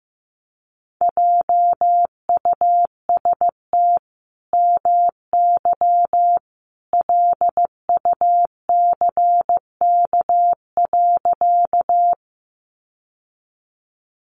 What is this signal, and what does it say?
Morse code, heard as a keyed tone, about 15 wpm, 700 Hz. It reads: JUST MY LUCK.
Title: Sentences from Top 1500 Words 15wpm